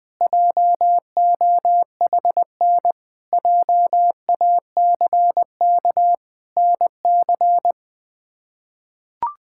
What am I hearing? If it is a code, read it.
Morse code, heard as a keyed tone, about 20 wpm, 700 Hz. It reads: JOHN JACK NC
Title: K1USN SST QSO Practice - 20wpm